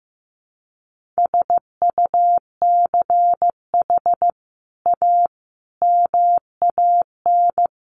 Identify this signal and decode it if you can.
Morse code, heard as a keyed tone, about 14 wpm, 700 Hz. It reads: SUCH A MAN
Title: Sentences from Top 1100 Words 15wpm